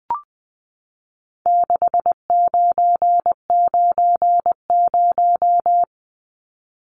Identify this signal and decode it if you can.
Morse code, heard as a keyed tone, about 20 wpm, 700 Hz. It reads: 6990
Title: Numbers - 4-Digit 20wpm